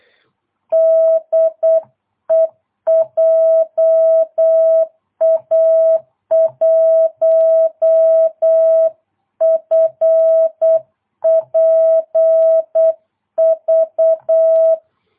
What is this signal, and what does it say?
Morse code, heard as a keyed tone, about 8 wpm, 650 Hz. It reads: DE JA1FPV